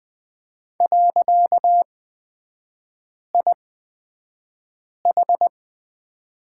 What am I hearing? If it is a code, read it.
Morse code, heard as a keyed tone, about 20 wpm, spaced as for 5 wpm, 690 Hz. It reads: .IH
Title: 255 Lesson 41e - Period - Three Characters 20wpm